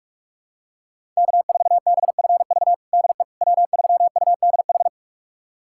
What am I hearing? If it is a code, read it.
Morse code, heard as a keyed tone, about 46 wpm, 700 Hz. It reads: K4BFV DE W3UDH
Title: USA Callsign DE Callsign - 5 Characters - 45wpm